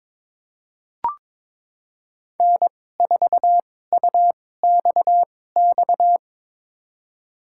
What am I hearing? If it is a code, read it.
Morse code, heard as a keyed tone, about 21 wpm, spaced as for 16 wpm, 700 Hz. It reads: N4UXX
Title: USA Call Signs - 2x Character Spacing - 22wpm